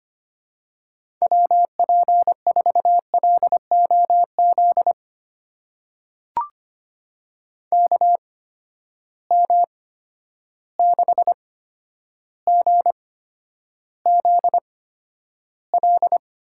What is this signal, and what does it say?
Morse code, heard as a keyed tone, about 25 wpm, spaced as for 11 wpm, 700 Hz. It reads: WP4LOZ KM6GZL
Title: USA Call Signs - 6 Characters - 8x Character Spacing - 25wpm